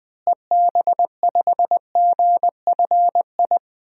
Morse code, heard as a keyed tone, about 20 wpm, 700 Hz. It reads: EB5GFI
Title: DXCC-100 Call Signs - 6 Characters - Repeated 2x - 20wpm